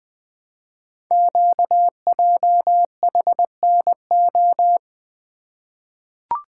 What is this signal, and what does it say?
Morse code, heard as a keyed tone, about 20 wpm, 700 Hz. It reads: QJHNO
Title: Code Groups - 5 Letters - 20wpm